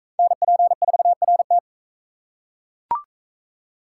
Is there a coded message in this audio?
Yes: Morse code from 0.2 to 1.6 s, heard as a keyed tone, about 37 wpm, 700 Hz. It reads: NP4RT